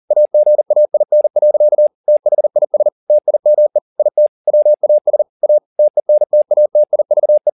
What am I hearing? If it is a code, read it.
Morse code, heard as a keyed tone, about 40 wpm, 600 Hz. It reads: AGAIN. THIS TIME IT WAS A TENTATIVE